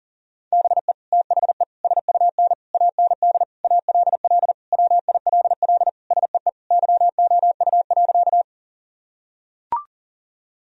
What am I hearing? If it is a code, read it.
Morse code, heard as a keyed tone, about 40 wpm, 700 Hz. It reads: BE THE SUN AND ALL WILL SEE YOU.
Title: Sentences from Top 1000 Words - Review Part 1 - 40wpm